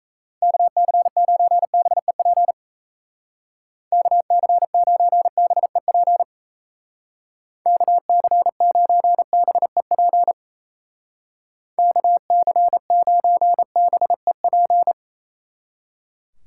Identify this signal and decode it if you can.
Morse code, heard as a keyed tone, about 35 wpm, 700 Hz. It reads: KC9BEP KC9BEP KC9BEP KC9BEP
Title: Speed Racing - USA Call Signs 28wpm